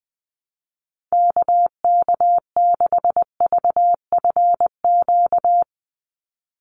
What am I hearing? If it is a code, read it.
Morse code, heard as a keyed tone, about 19 wpm, 700 Hz. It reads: KK6VFQ
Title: USA Call Signs 20wpm